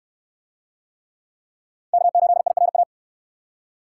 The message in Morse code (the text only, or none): K8IRA